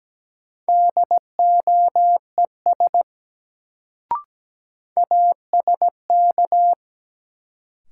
Morse code, heard as a keyed tone, about 17 wpm, 700 Hz. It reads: DOES ASK
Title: Sentences from Top 200 Words - New Word Drill - 17wpm